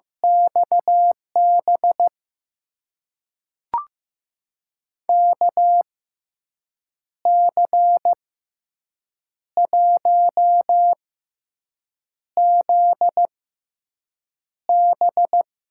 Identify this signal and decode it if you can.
Morse code, heard as a keyed tone, about 15 wpm, spaced as for 6 wpm, 700 Hz. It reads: XB KC1ZB
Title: USA Call Signs - 5 Characters - 6x Character Spacing - 15wpm